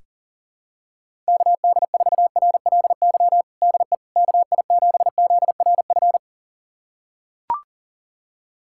Morse code, heard as a keyed tone, about 40 wpm, 700 Hz. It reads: KD4RLY DE KI7ZRF
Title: USA Callsign DE Callsign 40wpm